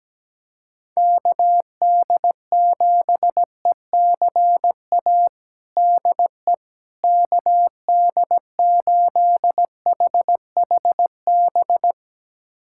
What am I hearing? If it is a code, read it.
Morse code, heard as a keyed tone, about 17 wpm, 700 Hz. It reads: KD7ECA DE KD8HHB
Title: USA Callsign DE Callsign - 6 Characters - 17wpm